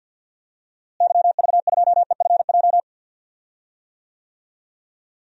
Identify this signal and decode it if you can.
Morse code, heard as a keyed tone, about 47 wpm, 700 Hz. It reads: YV2EFJ
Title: DXCC-100 Call Signs - 6 Characters - Repeated 2x - 50wpm